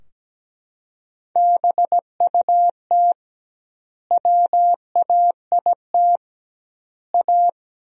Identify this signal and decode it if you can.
Morse code, heard as a keyed tone, about 17 wpm, 700 Hz. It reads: BUT WAIT A
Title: 2x Word-Spacing - Sentences from Top 500 Words - 17wpm